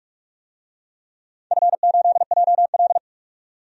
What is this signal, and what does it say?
Morse code, heard as a keyed tone, about 47 wpm, 700 Hz. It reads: F8JL
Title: DXCC-100 Call Signs - 4 Characters - 45wpm